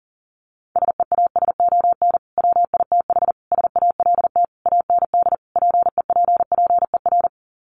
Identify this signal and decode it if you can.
Morse code, heard as a keyed tone, about 40 wpm, 700 Hz. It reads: SEASON WITH SALT AND PEPPER
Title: Sets of 5 Words 40wpm